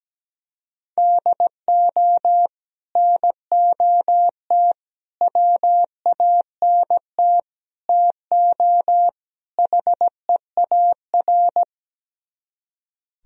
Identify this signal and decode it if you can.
Morse code, heard as a keyed tone, about 17 wpm, 700 Hz. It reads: DO NOT WANT TO HEAR
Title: Speed-Racing - Sets of 5 Words 17wpm